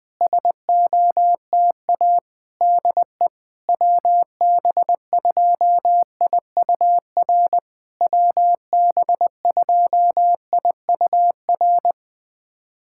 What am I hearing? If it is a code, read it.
Morse code, heard as a keyed tone, about 20 wpm, 700 Hz. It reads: SOTA DE WB2IUR WB2IUR
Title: SOTA QSO Practice 20wpm